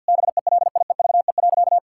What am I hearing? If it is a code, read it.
Morse code, heard as a keyed tone, about 50 wpm, 700 Hz. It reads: BELIEVE.
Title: Sentences from Top 1000 Words - Review Part 6 - 50wpm